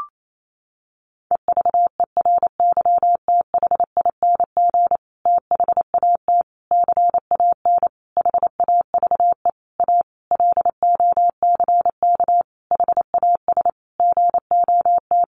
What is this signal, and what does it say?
Morse code, heard as a keyed tone, about 28 wpm, 700 Hz. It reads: EVERYTHING THAT CAN HAVE A LOCK HAS GOT
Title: Sentences from Top 1300 Words 28wpm